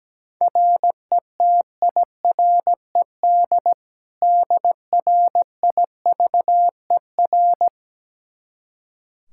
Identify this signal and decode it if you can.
Morse code, heard as a keyed tone, about 17 wpm, 700 Hz. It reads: RETIRED DRIVER
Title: K1USN SST QSO Practice 17wpm